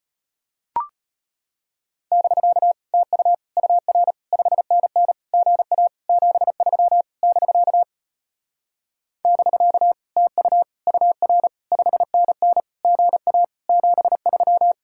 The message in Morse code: <BK> TU UR 5NN GA 73 <BK> <BK> TU UR 5NN GA 73

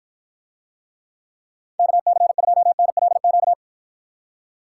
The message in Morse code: KC2NLQ

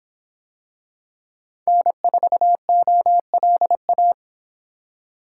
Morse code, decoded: N4OLA

